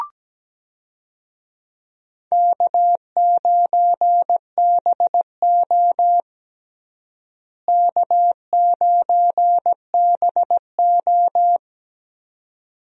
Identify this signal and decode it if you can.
Morse code, heard as a keyed tone, about 17 wpm, 700 Hz. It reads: K9BO K9BO
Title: USA Call Signs - 4 Characters - Repeated 2x - 17wpm